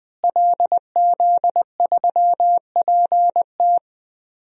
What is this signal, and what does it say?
Morse code, heard as a keyed tone, about 20 wpm, 700 Hz. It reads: LZ3PT
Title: DXCC-100 Call Signs - 5 Characters - Repeated 3x - 20wpm